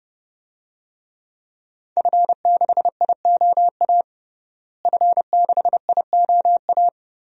Morse code, heard as a keyed tone, about 30 wpm, 700 Hz. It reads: F6IOA F6IOA
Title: DXCC-100 Call Signs - 5 Characters - Repeated 3x - 30wpm